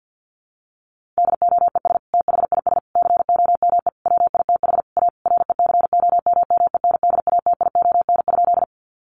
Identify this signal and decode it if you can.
Morse code, heard as a keyed tone, about 48 wpm, 700 Hz. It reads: DOES THIS COME WITH A RECOMMENDATION?
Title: Sentences from Top 1400 Words 50wpm